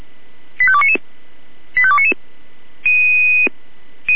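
Five-tone selective call (ZVEI), heard as quick runs of five tones, tones of about 70 ms. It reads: 85299 85299
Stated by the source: ZVEI five-tone selective call received on SDR.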